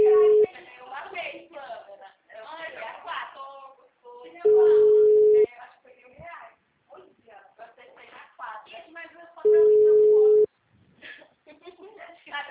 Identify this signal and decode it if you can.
Ringback tone (Europe, 425 Hz), with speech.
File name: out- -2009- - - .6984